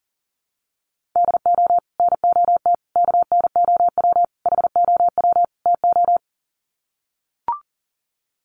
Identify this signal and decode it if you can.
Morse code, heard as a keyed tone, about 40 wpm, 700 Hz. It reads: DO NOT KNOW HOW TO
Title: Sets of 5 Words 40wpm